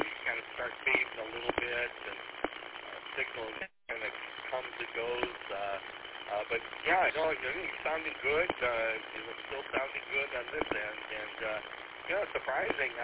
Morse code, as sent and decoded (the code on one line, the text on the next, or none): none